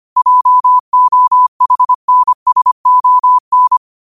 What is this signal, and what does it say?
Morse code, heard as a keyed tone, about 25 wpm, 1000 Hz. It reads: JOHNSON